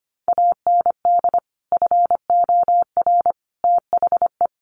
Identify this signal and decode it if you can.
Morse code, heard as a keyed tone, about 25 wpm, 700 Hz. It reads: AND FOR THE
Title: Sets of 5 Words 25wpm